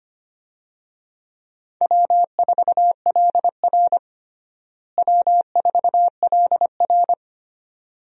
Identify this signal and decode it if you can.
Morse code, heard as a keyed tone, about 25 wpm, 700 Hz. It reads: W4LR W4LR
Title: USA Call Signs - 4 Characters - Repeated 2x - 25wpm